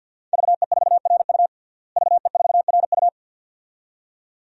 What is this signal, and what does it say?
Morse code, heard as a keyed tone, about 50 wpm, 700 Hz. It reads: VE4RU VE4RU
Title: DXCC-100 Call Signs - 5 Characters - Repeated 2x - 50wpm